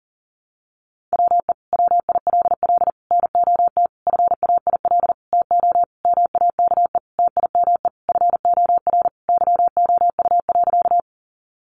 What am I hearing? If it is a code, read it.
Morse code, heard as a keyed tone, about 40 wpm, 700 Hz. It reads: WE WILL NOT FAIL TO MAKE TIME FOR YOU.
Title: Sentences from Top 1100 Words 40wpm